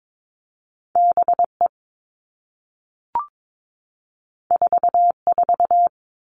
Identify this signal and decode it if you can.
Morse code, heard as a keyed tone, about 22 wpm, 700 Hz. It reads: B E 4 4
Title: Two Challenging Letter-Number Combinations 22wpm